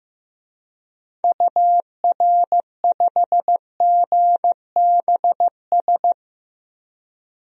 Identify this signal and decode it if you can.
Morse code, heard as a keyed tone, about 15 wpm, 700 Hz. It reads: UR5GBS
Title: DXCC-100 Call Signs - 6 Characters - Repeated 2x - 15wpm